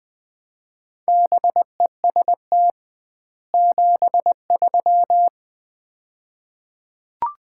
Morse code, heard as a keyed tone, about 20 wpm, 700 Hz. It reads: BEST 73
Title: Common QSO Phrases - 2x Word-Spacing - 20wpm